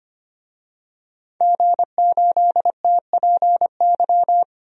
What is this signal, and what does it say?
Morse code, heard as a keyed tone, about 25 wpm, 700 Hz. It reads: G8TPY
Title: DXCC-100 Call Signs - 5 Characters - Repeated 3x - 25wpm